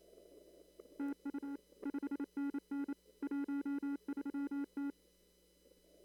Morse code, heard as a keyed tone, about 27 wpm, 280 Hz. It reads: TU 5NN 13T
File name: CW / kt5j wpx cw-op-k5pi